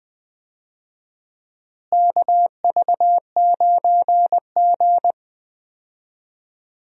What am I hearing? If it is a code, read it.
Morse code, heard as a keyed tone, about 20 wpm, 700 Hz. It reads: KV9G
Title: USA Call Signs - 4 Characters - Repeated 3x - 20wpm